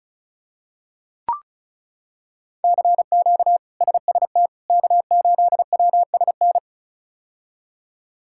Morse code, heard as a keyed tone, about 34 wpm, 700 Hz. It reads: CQ SST K8WSN
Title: K1USN SST QSO Practice - 35wpm